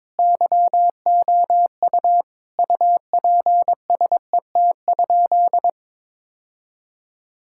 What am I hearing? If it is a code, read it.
Morse code, heard as a keyed tone, about 22 wpm, 700 Hz. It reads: YOU UPSET?
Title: Sentences from Top 1000 Words - Review Part 5 - 22wpm